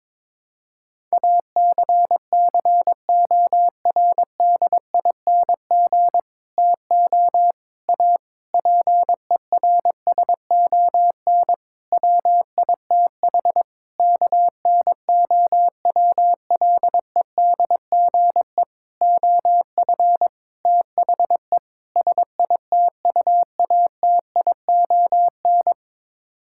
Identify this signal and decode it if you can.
Morse code, heard as a keyed tone, about 22 wpm, 700 Hz. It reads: ACCORDING TO A PERSON WITH KNOWLEDGE OF THE SITUATION
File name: Sets of 9 Words - 22wpm